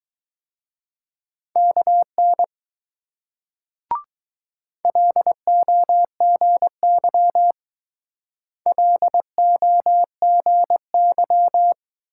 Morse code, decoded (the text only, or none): KN LOGY LOGY